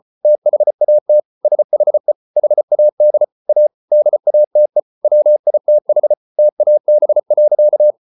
Morse code, heard as a keyed tone, about 34 wpm, 600 Hz. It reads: THAT SHE HAD A DATE WITH TAB.